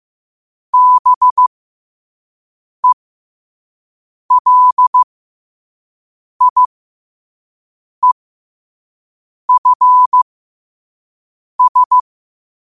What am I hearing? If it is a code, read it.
Morse code, heard as a keyed tone, about 15 wpm, spaced as for 5 wpm, 1000 Hz. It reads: BELIEFS
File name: qotdfiles / 5wpm